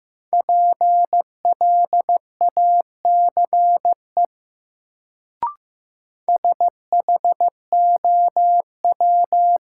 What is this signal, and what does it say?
Morse code, heard as a keyed tone, about 15 wpm, 700 Hz. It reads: PLACE SHOW
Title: Sentences from Top 200 Words - New Word Drill - 15wpm